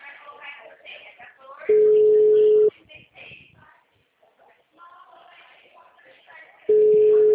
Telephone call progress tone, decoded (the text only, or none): ringback tone